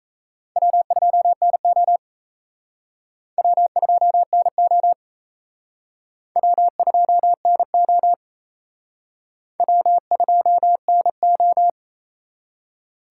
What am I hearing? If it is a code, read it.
Morse code, heard as a keyed tone, about 35 wpm, 700 Hz. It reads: W2NO W2NO W2NO W2NO